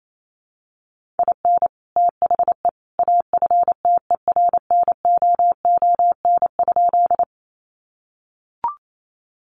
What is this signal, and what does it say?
Morse code, heard as a keyed tone, about 28 wpm, 700 Hz. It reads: IN THE AFTERNOON?
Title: Sentences from Top 1500 Words 28wpm